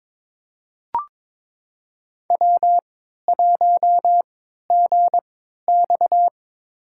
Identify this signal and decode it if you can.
Morse code, heard as a keyed tone, about 22 wpm, spaced as for 13 wpm, 700 Hz. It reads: W1GX
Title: USA Call Signs - 4 Characters - 3x Character Spacing - 22wpm